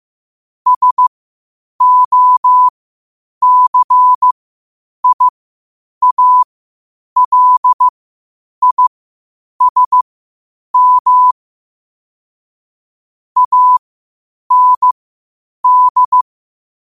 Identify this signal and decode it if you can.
Morse code, heard as a keyed tone, about 15 wpm, spaced as for 8 wpm, 1000 Hz. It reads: SOCIALISM AND
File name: qotdfiles / 8wpm